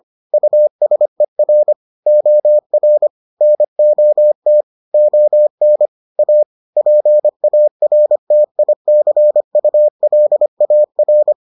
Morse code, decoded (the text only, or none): USER OR NOT ON A PARTICULAR